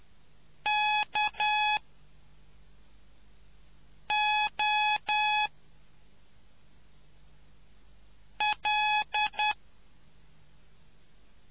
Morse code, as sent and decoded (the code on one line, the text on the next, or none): -.- --- .-..
KOL